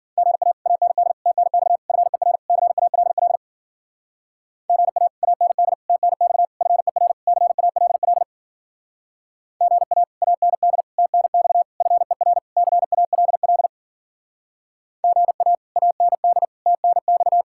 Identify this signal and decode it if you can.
Morse code, heard as a keyed tone, about 52 wpm, 700 Hz. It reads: GA AND TNX FER CALL GA AND TNX FER CALL GA AND TNX FER CALL GA AND TNX